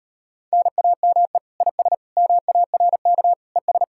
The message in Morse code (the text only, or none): NAME IS MARK ES